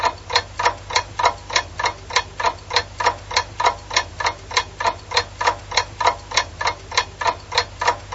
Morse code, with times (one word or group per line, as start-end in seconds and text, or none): none